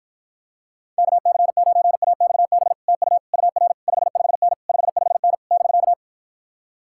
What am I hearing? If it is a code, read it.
Morse code, heard as a keyed tone, about 53 wpm, 700 Hz. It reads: KC9AXB TU UR 55N 55N <BK>